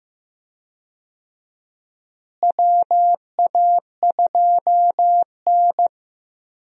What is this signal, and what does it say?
Morse code, heard as a keyed tone, about 15 wpm, 700 Hz. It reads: WA2N
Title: USA Call Signs - 4 Characters - Repeated 2x - 15wpm